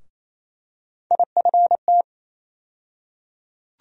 Morse code, heard as a keyed tone, about 28 wpm, 700 Hz. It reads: IFT
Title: USA Call Sign Postfixes - 3 Letters 28wpm